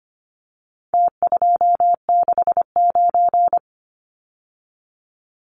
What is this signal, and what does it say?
Morse code, heard as a keyed tone, about 25 wpm, 700 Hz. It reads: T269
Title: Numbers - Contest Style - 4-Digit 25wpm